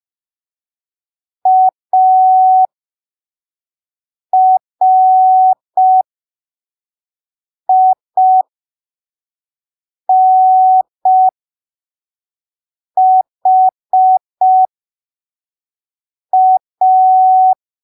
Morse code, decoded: ARINHA